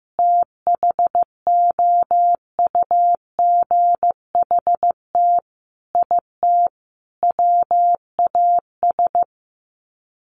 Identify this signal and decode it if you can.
Morse code, heard as a keyed tone, about 15 wpm, 700 Hz. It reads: THOUGHT IT WAS
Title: Sets of 4 Words 15wpm